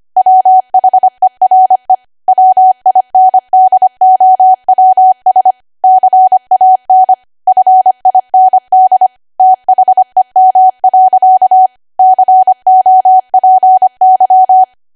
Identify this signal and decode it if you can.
Morse code, heard as a keyed tone, about 25 wpm, 750 Hz. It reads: WHERE WINDOWS CAN FIND THEM. COPY